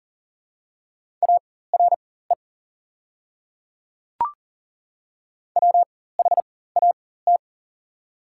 Morse code, heard as a keyed tone, about 40 wpm, spaced as for 19 wpm, 690 Hz. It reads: ARE WHAT
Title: 4x Character Spacing - Top 100 Words - 40wpm